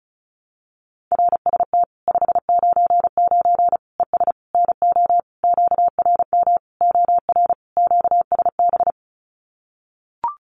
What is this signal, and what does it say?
Morse code, heard as a keyed tone, about 34 wpm, 700 Hz. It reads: RST 599 ES NO QRM OR QSB
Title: Common QSO Phrases 35wpm